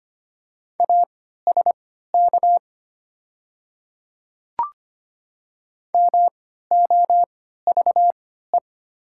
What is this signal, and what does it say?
Morse code, heard as a keyed tone, about 25 wpm, spaced as for 14 wpm, 700 Hz. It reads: ASK MOVE
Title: 3x Character Spacing - Top 300 Words - 25wpm